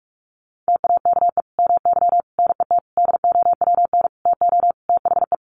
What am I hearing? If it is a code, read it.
Morse code, heard as a keyed tone, about 44 wpm, 700 Hz. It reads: TAKE MY NET DOWN TO THE
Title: Sentences from Top 1400 Words 45wpm